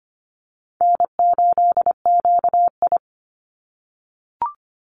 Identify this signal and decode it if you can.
Morse code, heard as a keyed tone, about 25 wpm, 700 Hz. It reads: N8QI
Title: USA Call Signs 25wpm